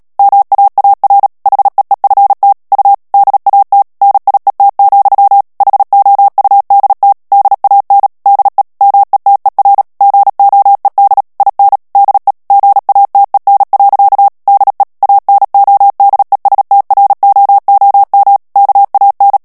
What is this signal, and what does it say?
Morse code, heard as a keyed tone, about 37 wpm, 800 Hz. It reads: MAAR HEEFT U DAT NIET, HOUDT DAN DE METER GOED IN DE GATEN. DE ANODESTROOM KAN
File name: morse mp3 / HOLLANDAIS texte clair 2 vitesse 35 mots minute